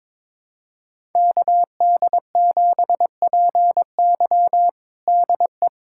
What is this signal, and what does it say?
Morse code, heard as a keyed tone, about 22 wpm, 700 Hz. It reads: KD7PY DE
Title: USA Callsign DE Callsign - 5 Characters - 22wpm